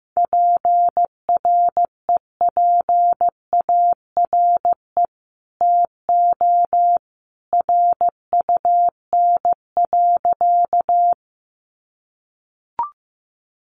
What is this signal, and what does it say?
Morse code, heard as a keyed tone, about 15 wpm, 700 Hz. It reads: PREPARE TO RUN.
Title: Sentences from Top 900 Words 15wpm